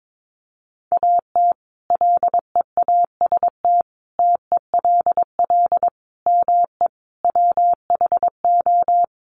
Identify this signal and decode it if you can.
Morse code, heard as a keyed tone, about 22 wpm, 700 Hz. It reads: AT LEAST TELL ME WHO
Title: Sentences from Top 700 Words 22wpm